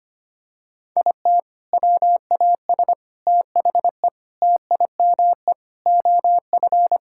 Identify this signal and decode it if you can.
Morse code, heard as a keyed tone, about 25 wpm, 700 Hz. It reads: IT WAS THE TIME OF
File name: practice / 25wpm / Sentences with 4-Word Phrases - 25wpm